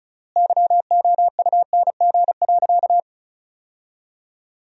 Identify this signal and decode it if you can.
Morse code, heard as a keyed tone, about 36 wpm, 700 Hz. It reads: YOUNG.
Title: 3x Word-Spacing - Sentences from Top 400 Words - 35wpm